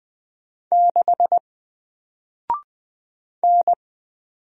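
Morse code, heard as a keyed tone, about 20 wpm, 700 Hz. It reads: 6 N
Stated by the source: Single letter and number Morse drill at 20 wpm, from N4LCD.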